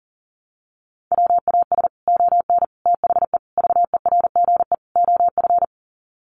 Morse code, heard as a keyed tone, about 40 wpm, 700 Hz. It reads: WAS ON THE VERGE OF
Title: Sets of 5 Words Encore 40wpm